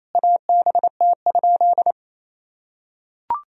Morse code, heard as a keyed tone, about 28 wpm, 700 Hz. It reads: ABT?
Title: Sentences from Top 100 Words and Abbreviations 28wpm